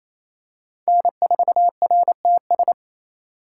Morse code, heard as a keyed tone, about 28 wpm, 700 Hz. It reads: N4RTS
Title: USA Call Signs - 6x Character Spacing - 28wpm